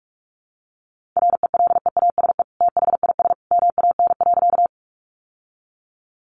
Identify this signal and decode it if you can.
Morse code, heard as a keyed tone, about 44 wpm, 700 Hz. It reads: RELEASE THIS MAN.
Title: Sentences from Top 1100 Words 45wpm